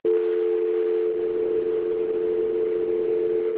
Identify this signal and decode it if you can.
Dial tone (North America), 350 + 440 Hz.